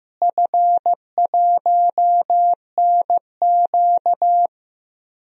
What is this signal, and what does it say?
Morse code, heard as a keyed tone, about 15 wpm, 700 Hz. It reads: F1NQ
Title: DXCC-100 Call Signs - 4 Characters - Repeated 2x - 15wpm